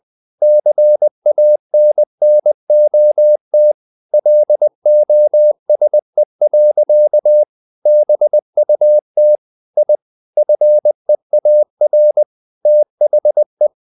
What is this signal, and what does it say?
Morse code, heard as a keyed tone, about 20 wpm, 600 Hz. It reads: CANNOT LOSE. BUT I FEAR THE
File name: the old man and the sea / the old man and the sea part 1 20wpm